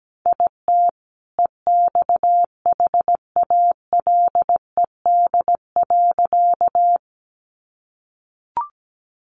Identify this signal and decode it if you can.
Morse code, heard as a keyed tone, about 17 wpm, 700 Hz. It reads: IT EXHALED.
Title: A Sound Of Thunder 17wpm